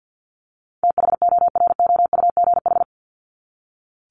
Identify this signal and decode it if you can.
Morse code, heard as a keyed tone, about 50 wpm, 700 Hz. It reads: THOROUGH
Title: LY Suffix - Top 100 Words with Base Words 50wpm